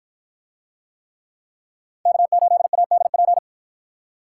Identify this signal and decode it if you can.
Morse code, heard as a keyed tone, about 47 wpm, 700 Hz. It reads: K8ADP